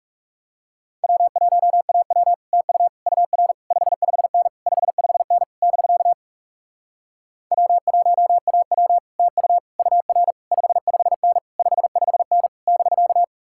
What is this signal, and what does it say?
Morse code, heard as a keyed tone, about 40 wpm, 700 Hz. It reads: W1AW TU UR 55N 55N <BK> W1AW TU UR 55N 55N <BK>